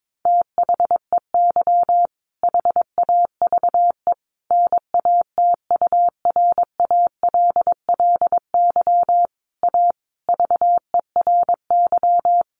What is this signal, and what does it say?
Morse code, heard as a keyed tone, about 22 wpm, 700 Hz. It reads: THEY HAVE NATURALLY A VERY
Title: Sentences from Top 1300 Words 22wpm